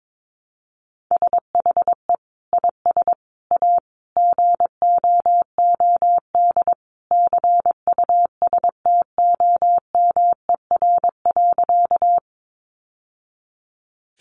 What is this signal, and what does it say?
Morse code, heard as a keyed tone, about 22 wpm, 700 Hz. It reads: SHE IS A GOOD CUSTOMER.
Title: Sentences from Top 1100 Words 22wpm